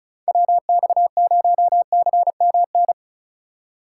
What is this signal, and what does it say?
Morse code, heard as a keyed tone, about 34 wpm, 700 Hz. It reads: WX0CMN
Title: USA Call Signs - 6 Characters - 3x Character Spacing - 35wpm